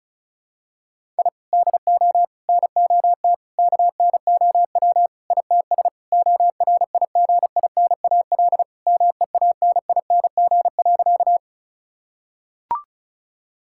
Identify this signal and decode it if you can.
Morse code, heard as a keyed tone, about 36 wpm, 700 Hz. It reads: I DO NOT KNOW ITS ORIGINAL MEANING.
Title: Sentences from Top 1000 Words - Review Part 6 - 35wpm